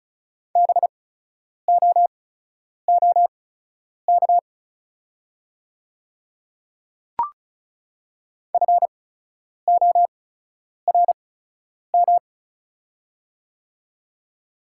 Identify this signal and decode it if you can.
Morse code, heard as a keyed tone, about 36 wpm, spaced as for 10 wpm, 700 Hz. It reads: BOOK FORM
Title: 8x Character Spacing - Top 300 Words - 35wpm